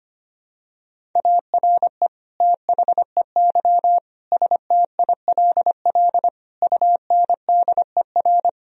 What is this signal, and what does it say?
Morse code, heard as a keyed tone, about 25 wpm, 700 Hz. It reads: ARE THEY STILL UNDER
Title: Sentences from Top 1000 Words - Review Part 1 - 25wpm